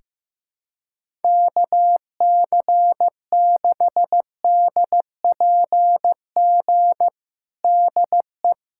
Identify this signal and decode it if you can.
Morse code, heard as a keyed tone, about 15 wpm, 700 Hz. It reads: KC6DPG DE
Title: USA Callsign DE Callsign - 6 Characters - 15wpm